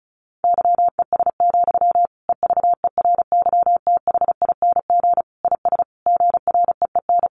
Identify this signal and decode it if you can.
Morse code, heard as a keyed tone, about 34 wpm, 700 Hz. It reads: YES, EVERYTHING IS GREEN
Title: Sentences from Top 700 Words 35wpm